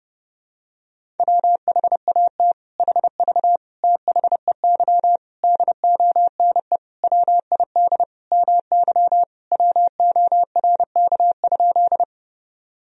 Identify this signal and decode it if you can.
Morse code, heard as a keyed tone, about 30 wpm, 700 Hz. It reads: WHAT HV THEY DONE WID MY WORK?
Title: Sentences from Top 500 Words and Abbreviations 30wpm